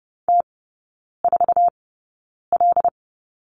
Morse code, heard as a keyed tone, about 30 wpm, spaced as for 9 wpm, 700 Hz. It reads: T4L
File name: course / 30:10wpm / 087 Lesson 15e - Number 5 - Three Characters 30wpm